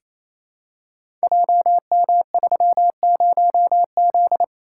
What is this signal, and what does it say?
Morse code, heard as a keyed tone, about 28 wpm, 700 Hz. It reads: JM30Z
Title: Code Groups - 5 Letter:Numbers - 28wpm